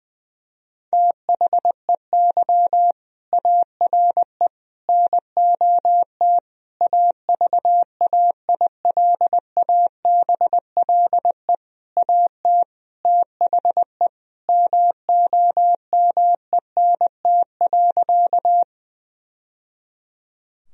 Morse code, heard as a keyed tone, about 20 wpm, 700 Hz. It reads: THEY ARE NOT AVAILABLE AT THE MOMENT.
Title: Sentences from Top 1000 Words - Review Part 3 - 20wpm